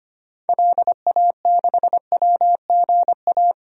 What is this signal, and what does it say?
Morse code, heard as a keyed tone, about 25 wpm, 700 Hz. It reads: LA6WGA
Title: DXCC-100 Call Signs - 6 Characters - Repeated 2x - 25wpm